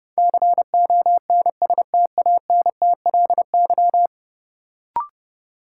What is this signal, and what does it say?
Morse code, heard as a keyed tone, about 30 wpm, 700 Hz. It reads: CONSTANTLY
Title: Sentences from Top 1000 Words - Review Part 7 - New Word Drill - 30wpm